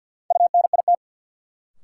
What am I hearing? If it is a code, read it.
Morse code, heard as a keyed tone, about 53 wpm, 700 Hz. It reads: UNIT
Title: Sentences from Top 1000 Words - Review Part 2 - New Word Drill - 50wpm